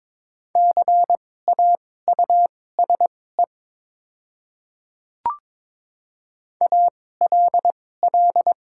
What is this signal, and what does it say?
Morse code, heard as a keyed tone, about 22 wpm, spaced as for 16 wpm, 700 Hz. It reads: CAUSE ALL
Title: 2x Character Spacing - Top 200 Words - 22wpm